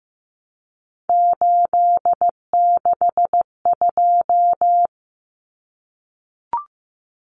Morse code, heard as a keyed tone, about 15 wpm, 700 Hz. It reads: 862
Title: Common Cut Numbers - 3-Digit 15wpm